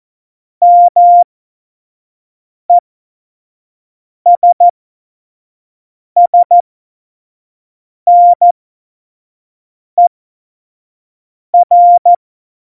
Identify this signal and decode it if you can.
Morse code, heard as a keyed tone, about 14 wpm, spaced as for 5 wpm, 700 Hz. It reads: MESSNER
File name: Morsekurs / Witze / Witz 26. Januar 2021, 25 F70